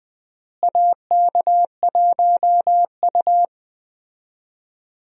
Morse code, heard as a keyed tone, about 20 wpm, 700 Hz. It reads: AK1U
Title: USA Call Signs - 4 Characters - Repeated 2x - 20wpm